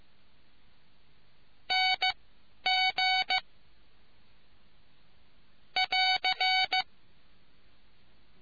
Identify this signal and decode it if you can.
Morse code, heard as a keyed tone, about 15 wpm, spaced as for 10 wpm, 3000 Hz. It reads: NG <AR>